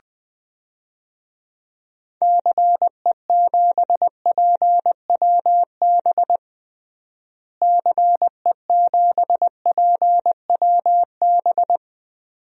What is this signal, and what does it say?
Morse code, heard as a keyed tone, about 20 wpm, 700 Hz. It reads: CE7PWB CE7PWB
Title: DXCC-100 Call Signs - 6 Characters - Repeated 2x - 20wpm